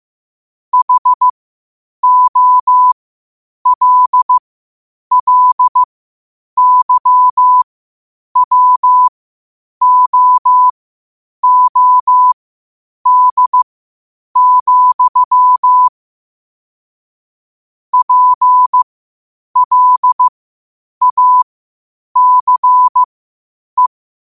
Morse code, heard as a keyed tone, about 15 wpm, spaced as for 8 wpm, 1000 Hz. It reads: HOLLYWOOD, PLACE
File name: qotdfiles / 8wpm